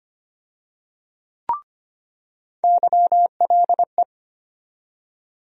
Morse code, heard as a keyed tone, about 25 wpm, 700 Hz. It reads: YLE
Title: USA Call Sign Postfixes - 3 Letters 25wpm